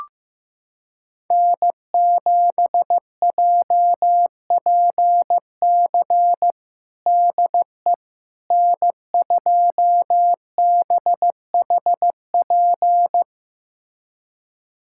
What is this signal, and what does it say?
Morse code, heard as a keyed tone, about 15 wpm, 700 Hz. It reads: N7JPC DE N2BHP
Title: USA Callsign DE Callsign - 5 Characters - 15wpm